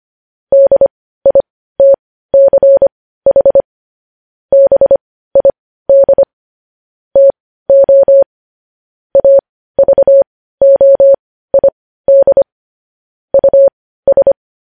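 Morse code, heard as a keyed tone, about 25 wpm, spaced as for 15 wpm, 550 Hz. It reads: DITCH BID TO AVOID US
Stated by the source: Morse code news headlines at 15 wpm.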